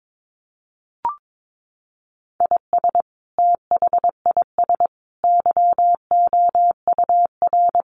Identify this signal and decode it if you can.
Morse code, heard as a keyed tone, about 22 wpm, 700 Hz. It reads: IS THIS YOUR
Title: Sentences from Top 1500 Words 22wpm